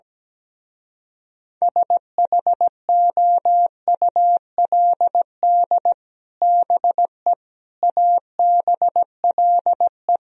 Morse code, heard as a keyed tone, about 17 wpm, 700 Hz. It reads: SHOULD BE ABLE
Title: Speed-Racing - Sets of 4 Words 17wpm